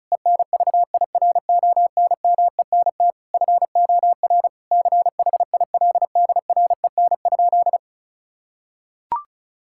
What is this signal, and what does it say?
Morse code, heard as a keyed tone, about 36 wpm, 700 Hz. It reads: ENVIRONMENT FOR CHILDREN?
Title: Sentences from Top 1000 Words - Review Part 5 - 35wpm